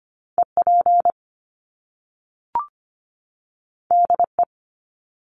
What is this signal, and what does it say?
Morse code, heard as a keyed tone, about 25 wpm, 700 Hz. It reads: EP DE
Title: Top Two Letter Combinations in Words 25wpm